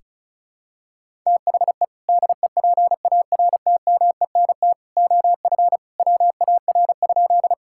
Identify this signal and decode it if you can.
Morse code, heard as a keyed tone, about 34 wpm, 700 Hz. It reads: THE DEPARTMENT OF WAR?
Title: Sentences from Top 1000 Words - Review Part 5 - 35wpm